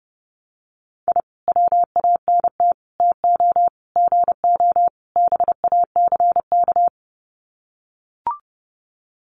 Morse code, heard as a keyed tone, about 30 wpm, 700 Hz. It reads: I WANT TO GO BACK
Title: Sets of 5 Words Encore 30wpm